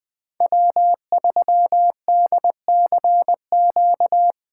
Morse code, heard as a keyed tone, about 25 wpm, 700 Hz. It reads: W3DCQ
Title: Speed Racing - USA 1x3 Call Signs 20wpm